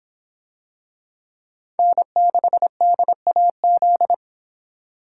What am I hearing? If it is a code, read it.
Morse code, heard as a keyed tone, about 26 wpm, 700 Hz. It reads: N6DAZ